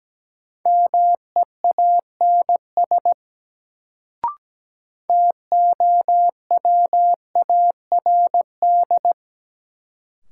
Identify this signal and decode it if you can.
Morse code, heard as a keyed tone, about 17 wpm, 700 Hz. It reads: MEANS TOWARD
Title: Sets of 5 Words - New Word Drill - 17wpm